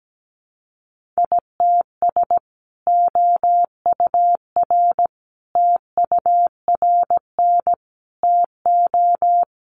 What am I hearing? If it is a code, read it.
Morse code, heard as a keyed tone, about 17 wpm, 700 Hz. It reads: ITS OUR TURN TO
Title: Sentences from Top 600 Words 17wpm